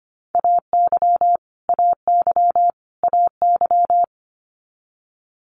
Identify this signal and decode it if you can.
Morse code, heard as a keyed tone, about 25 wpm, 700 Hz. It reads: AY AY AY
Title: Multi-letter Phonemes Repeated 3x 25wpm